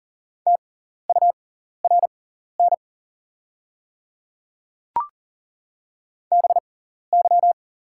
Morse code, heard as a keyed tone, about 40 wpm, spaced as for 14 wpm, 690 Hz. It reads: TURN BY
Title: 6x Character Spacing - Top 200 Words - 40wpm